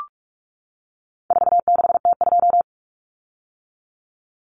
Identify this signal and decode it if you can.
Morse code, heard as a keyed tone, about 44 wpm, 700 Hz. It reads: 46T2
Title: Common Cut Numbers - 4-Digit 45wpm